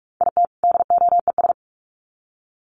Morse code, heard as a keyed tone, about 44 wpm, 700 Hz. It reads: IT DOES
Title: Sets of 2 Words 45wpm